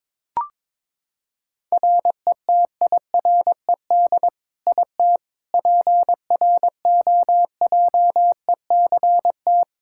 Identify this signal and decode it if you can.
Morse code, heard as a keyed tone, about 22 wpm, 700 Hz. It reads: RETIRED IT PROJECT
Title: Practice QSO - Exchange 3 - More Info - 22wpm